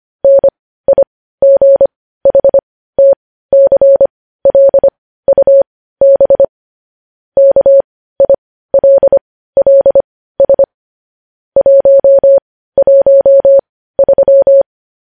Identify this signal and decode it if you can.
Morse code, heard as a keyed tone, about 25 wpm, spaced as for 15 wpm, 550 Hz. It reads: NIGHTCLUB KILLS 113
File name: podcast / News headlines-1x-25x15